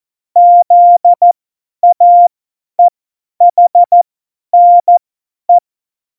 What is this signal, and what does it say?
Morse code, heard as a keyed tone, about 14 wpm, spaced as for 10 wpm, 700 Hz. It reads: ZAEHNE
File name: Morsekurs / Witze / Witz 14 Juli 2020, 50 F70